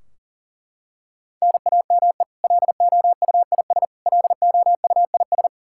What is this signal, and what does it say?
Morse code, heard as a keyed tone, about 40 wpm, 700 Hz. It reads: NAME LOUIS LOUIS
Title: K1USN SST QSO Practice 40wpm